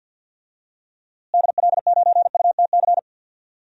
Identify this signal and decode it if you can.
Morse code, heard as a keyed tone, about 47 wpm, 700 Hz. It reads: DL9UTC
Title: DXCC-100 Call Signs - 6 Characters - Repeated 3x - 50wpm